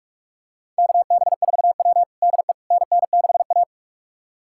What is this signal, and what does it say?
Morse code, heard as a keyed tone, about 41 wpm, 700 Hz. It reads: KB4W DE NN6A
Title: USA Callsign DE Callsign - 4 Characters - 45wpm